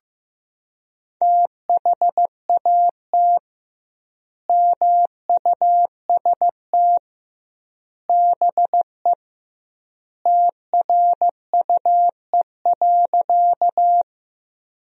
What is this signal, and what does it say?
Morse code, heard as a keyed tone, about 15 wpm, 700 Hz. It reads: THAT MUST BE TRUE.
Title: 2x Word-Spacing - Sentences from Top 400 Words - 15wpm